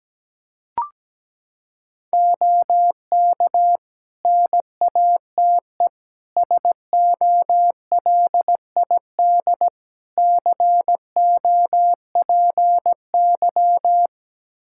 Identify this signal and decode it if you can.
Morse code, heard as a keyed tone, about 17 wpm, 700 Hz. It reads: OK NATE SOLID COPY
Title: Practice QSO - Exchange 3 - More Info - 17wpm